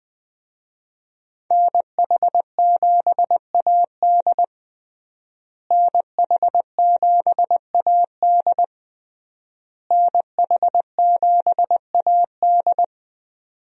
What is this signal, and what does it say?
Morse code, heard as a keyed tone, about 21 wpm, 700 Hz. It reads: NH7AD NH7AD NH7AD
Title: USA Call Signs - 5 Characters - Repeated 3x - 20wpm